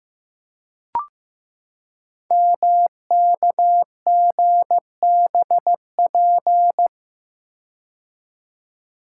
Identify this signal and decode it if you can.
Morse code, heard as a keyed tone, about 15 wpm, 700 Hz. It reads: MKGBP
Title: Code Groups - 5 Letters - 15wpm